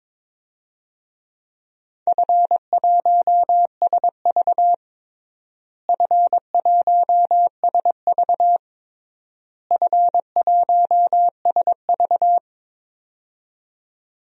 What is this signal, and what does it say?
Morse code, heard as a keyed tone, about 22 wpm, 700 Hz. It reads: F1SV F1SV F1SV
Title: DXCC-100 Call Signs - 4 Characters - Repeated 3x - 22wpm